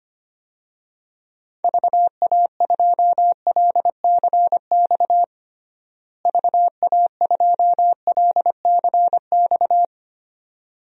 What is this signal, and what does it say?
Morse code, heard as a keyed tone, about 25 wpm, 700 Hz. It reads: VA2LCX VA2LCX
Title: DXCC-100 Call Signs - 6 Characters - Repeated 2x - 25wpm